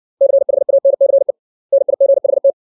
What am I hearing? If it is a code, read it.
Morse code, heard as a keyed tone, about 60 wpm, 550 Hz. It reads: CHANCE NIGHT
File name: 632 words / 632 Words 60WPM